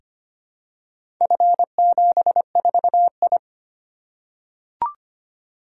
Morse code, heard as a keyed tone, about 25 wpm, 700 Hz. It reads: F74I
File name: practice / 25wpm / Four Letter-Number Combinations - 25wpm